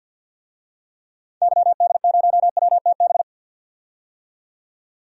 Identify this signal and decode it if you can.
Morse code, heard as a keyed tone, about 46 wpm, 700 Hz. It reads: YD0WTB
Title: DXCC-100 Call Signs - 6 Characters - 50wpm